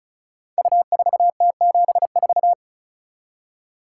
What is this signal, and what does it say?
Morse code, heard as a keyed tone, about 36 wpm, 700 Hz. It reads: U4T74
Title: Code Groups - 5 Letter:Numbers - 35wpm